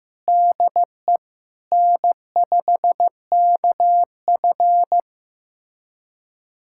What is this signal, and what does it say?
Morse code, heard as a keyed tone, about 16 wpm, 700 Hz. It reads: DE N5KF
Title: USA Callsign DE Callsign - 4 Characters - 15wpm